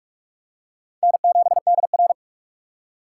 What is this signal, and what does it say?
Morse code, heard as a keyed tone, about 46 wpm, 700 Hz. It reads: N7DR
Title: K1USN SST QSO Practice - 45wpm